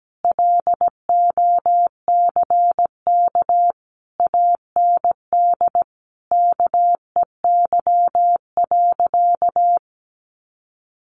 Morse code, heard as a keyed tone, about 17 wpm, 700 Hz. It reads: LOCK AND KEY.
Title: Sentences from Top 1400 Words 17wpm